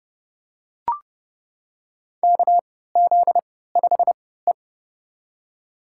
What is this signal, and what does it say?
Morse code, heard as a keyed tone, about 32 wpm, spaced as for 17 wpm, 700 Hz. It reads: KZ5E
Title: USA Call Signs - 4 Characters - 3x Character Spacing - 30wpm